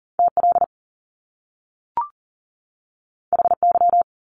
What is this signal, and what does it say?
Morse code, heard as a keyed tone, about 40 wpm, 700 Hz. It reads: TL HY
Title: Top Two Letter Combinations in Words 40wpm